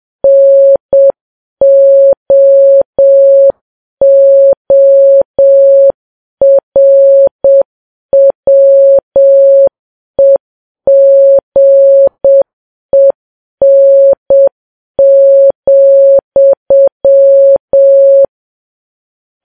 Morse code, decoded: NOORWEGEN,